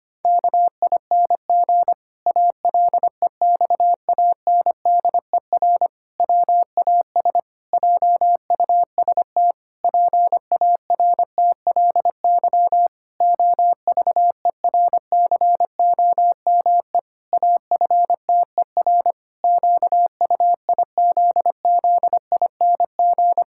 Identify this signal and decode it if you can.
Morse code, heard as a keyed tone, about 25 wpm, 700 Hz. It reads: KING ALEXANDER WAS JUST PARTLY OVERCOME AFTER QUIZZING
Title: Pangram Challenge v2 - 25wpm